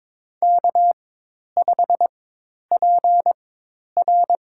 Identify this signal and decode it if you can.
Morse code, heard as a keyed tone, about 22 wpm, spaced as for 10 wpm, 700 Hz. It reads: K5PR
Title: USA Call Signs - 4 Characters - 4x Character Spacing - 22wpm